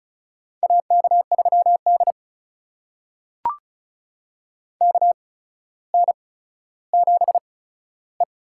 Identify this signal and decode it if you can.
Morse code, heard as a keyed tone, about 36 wpm, spaced as for 15 wpm, 700 Hz. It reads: AK3D KN7E
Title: USA Call Signs - 4 Characters - 8x Character Spacing - 35wpm